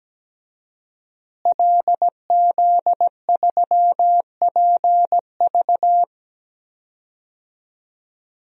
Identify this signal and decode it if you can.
Morse code, heard as a keyed tone, about 17 wpm, 700 Hz. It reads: LZ3PV
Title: DXCC-100 Call Signs - 5 Characters - Repeated 2x - 17wpm